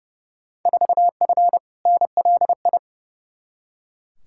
Morse code, heard as a keyed tone, about 30 wpm, 700 Hz. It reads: 4F NLI
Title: Field Day Exchanges Encore - 30wpm